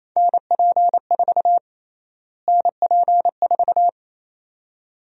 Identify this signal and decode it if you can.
Morse code, heard as a keyed tone, about 27 wpm, 700 Hz. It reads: NP4 NP4
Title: USA Call Sign Prefixes - Repeated 2x - 28wpm